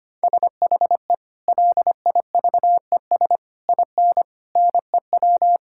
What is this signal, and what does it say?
Morse code, heard as a keyed tone, about 25 wpm, 700 Hz. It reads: SHE LIVES IN NEW
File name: practice / 25wpm / Sentences with 3-Word Phrases - 25wpm